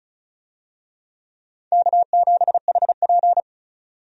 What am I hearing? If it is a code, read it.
Morse code, heard as a keyed tone, about 34 wpm, 700 Hz. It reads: K7HP
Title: USA Call Signs - 4 Characters - 35wpm